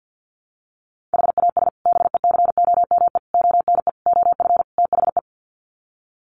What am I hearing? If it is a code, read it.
Morse code, heard as a keyed tone, about 50 wpm, 700 Hz. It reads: HAS BECOME ONE OF THE
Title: Sets of 5 Words Encore 50wpm